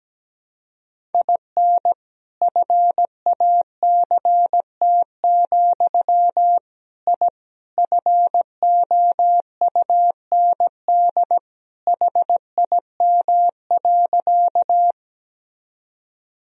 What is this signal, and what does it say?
Morse code, heard as a keyed tone, about 17 wpm, 700 Hz. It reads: IN FACT, I FOUND HIM.
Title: Sentences from Top 1000 Words - Review Part 3 - 17wpm